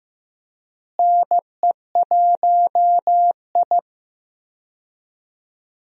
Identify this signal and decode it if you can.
Morse code, heard as a keyed tone, about 15 wpm, 700 Hz. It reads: NE1I
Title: USA Call Signs - 4 Characters - Repeated 2x - 15wpm